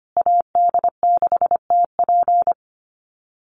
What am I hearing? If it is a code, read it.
Morse code, heard as a keyed tone, about 25 wpm, 700 Hz. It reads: AD6TP
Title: USA Call Signs Repeated 3x 25wpm